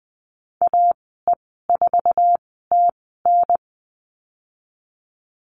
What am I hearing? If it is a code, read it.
Morse code, heard as a keyed tone, about 20 wpm, spaced as for 14 wpm, 700 Hz. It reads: AE4TN
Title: 048 Lesson 9g - Number 4 - Call Signs 20wpm